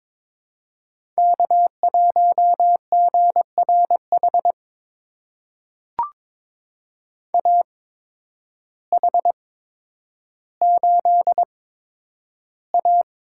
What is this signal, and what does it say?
Morse code, heard as a keyed tone, about 22 wpm, spaced as for 10 wpm, 700 Hz. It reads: K1GRH AH8A
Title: USA Call Signs - 8x Character Spacing - 22wpm